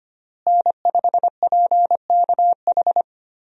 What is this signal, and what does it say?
Morse code, heard as a keyed tone, about 25 wpm, 700 Hz. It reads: N5PKH
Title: USA Call Signs - 5 Characters - Repeated 2x - 25wpm